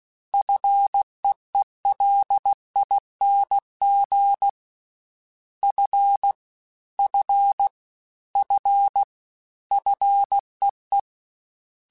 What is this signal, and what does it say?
Morse code, heard as a keyed tone, about 16 wpm, 800 Hz. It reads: FEELING F F F FEE
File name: Combinations / Combinations-mp3s / Combinations@15